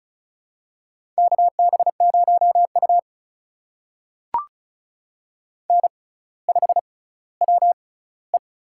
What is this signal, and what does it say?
Morse code, heard as a keyed tone, about 36 wpm, spaced as for 18 wpm, 700 Hz. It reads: KB0U N5WE
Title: USA Call Signs - 4 Characters - 6x Character Spacing - 35wpm